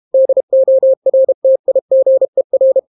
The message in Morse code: DORTIGER